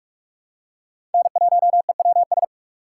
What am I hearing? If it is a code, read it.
Morse code, heard as a keyed tone, about 43 wpm, 700 Hz. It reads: N1EWS
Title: USA Call Signs - 5 Characters - 4x Character Spacing - 45wpm